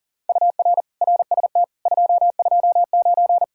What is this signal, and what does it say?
Morse code, heard as a keyed tone, about 48 wpm, 700 Hz. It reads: UR RST 229
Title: Speed-Racing Common QSO Phrases - 40wpm